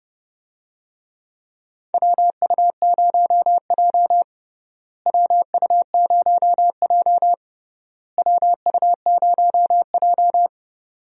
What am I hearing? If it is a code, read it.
Morse code, heard as a keyed tone, about 30 wpm, 700 Hz. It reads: WU0J WU0J WU0J
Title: USA Call Signs - 4 Characters - Repeated 3x - 30wpm